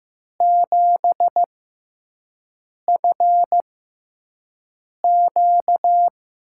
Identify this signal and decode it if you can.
Morse code, heard as a keyed tone, about 15 wpm, spaced as for 5 wpm, 700 Hz. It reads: 7FQ
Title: USA Call Signs - 5 Characters - 6x Character Spacing - 15wpm